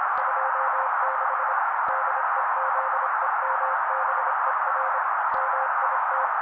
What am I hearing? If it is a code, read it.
Morse code, heard as a keyed tone, about 26 wpm, 530 Hz. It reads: 26 DEZEMBER MIT
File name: daily / 25wpm noise